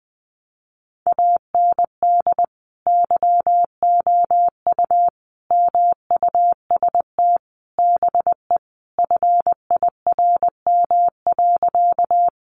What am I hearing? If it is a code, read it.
Morse code, heard as a keyed tone, about 20 wpm, 700 Hz. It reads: AND YOU MUST BE FIRM.
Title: Sentences from Top 1400 Words 20wpm